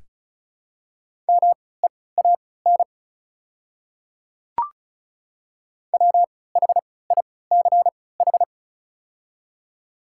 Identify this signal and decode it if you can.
Morse code, heard as a keyed tone, about 34 wpm, spaced as for 20 wpm, 690 Hz. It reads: MEAN WHICH
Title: 3x Character Spacing - Top 500 Words - 35wpm